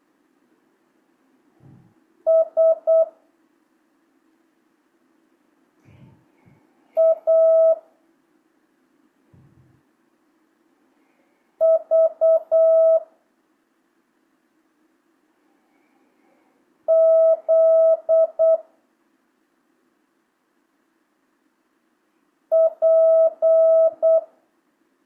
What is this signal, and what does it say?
Morse code, heard as a keyed tone, about 8 wpm, 1300 Hz. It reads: S A V Z P
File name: jh8cbh / 2025cw